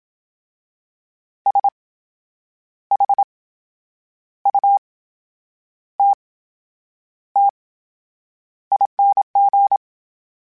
Morse code, heard as a keyed tone, about 27 wpm, spaced as for 8 wpm, 800 Hz. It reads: SHUTTING
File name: Special INGs / Special INGs-mp3s / VerySlowToFaster Ing @25